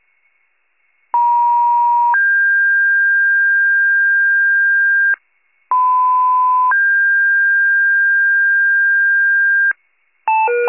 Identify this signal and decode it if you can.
Two-tone page, heard as a one-second tone, then a three-second tone, about 950 then 1600 Hz, about 1010 then 1600 Hz. It is heard from 1.1 to 9.7 s.